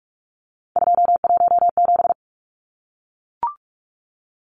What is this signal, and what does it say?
Morse code, heard as a keyed tone, about 44 wpm, 700 Hz. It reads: 217
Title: Numbers - 3-Digit 45wpm